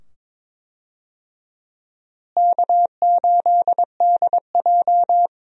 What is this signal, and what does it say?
Morse code, heard as a keyed tone, about 22 wpm, 700 Hz. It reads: K8DJ
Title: USA Call Signs - 4 Characters - Repeated 2x - 22wpm